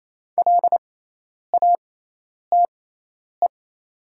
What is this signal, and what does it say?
Morse code, heard as a keyed tone, about 28 wpm, spaced as for 10 wpm, 700 Hz. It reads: LATE
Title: 6x Character Spacing - Top 300 Words - 28wpm